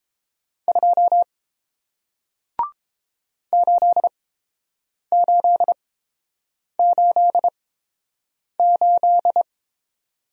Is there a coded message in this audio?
Yes: Morse code from 0.7 to 9.4 s, heard as a keyed tone, about 30 wpm, spaced as for 7 wpm, 700 Hz. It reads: 2 8888